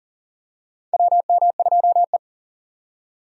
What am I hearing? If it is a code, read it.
Morse code, heard as a keyed tone, about 40 wpm, 700 Hz. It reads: WM2E
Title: USA Call Signs - 4 Characters - 40wpm